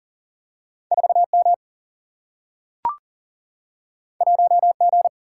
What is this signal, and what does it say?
Morse code, heard as a keyed tone, about 40 wpm, 700 Hz. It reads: 4M 1G
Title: Single-Letter Single-Number 40wpm